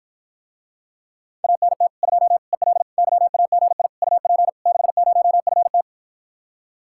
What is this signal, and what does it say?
Morse code, heard as a keyed tone, about 54 wpm, 700 Hz. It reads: ANT 2 EL YAGI UP 60FT